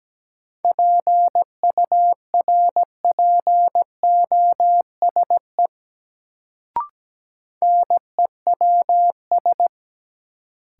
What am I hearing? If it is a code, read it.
Morse code, heard as a keyed tone, about 17 wpm, 700 Hz. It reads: PURPOSE NEWS
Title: Sets of 5 Words - New Word Drill - 17wpm